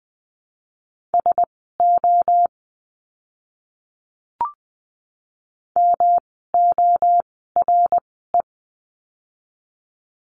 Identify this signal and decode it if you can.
Morse code, heard as a keyed tone, about 20 wpm, spaced as for 15 wpm, 700 Hz. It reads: SO MORE
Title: 160 Lesson 26f - Letter g - Words 20wpm